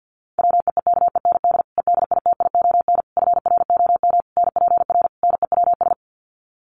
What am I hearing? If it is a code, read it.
Morse code, heard as a keyed tone, about 50 wpm, 700 Hz. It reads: WEEKEND EDITION FROM NPR NEWS
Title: Sets of 5 Words 50wpm